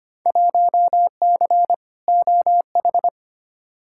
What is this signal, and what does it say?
Morse code, heard as a keyed tone, about 25 wpm, 700 Hz. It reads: 1C OH
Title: Field Day Exchanges - 25wpm